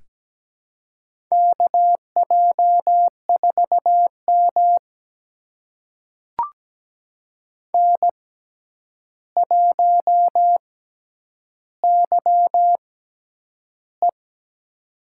Morse code, heard as a keyed tone, about 17 wpm, spaced as for 9 wpm, 700 Hz. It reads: KJ4M N1YE
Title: USA Call Signs - 4 Characters - 6x Character Spacing - 17wpm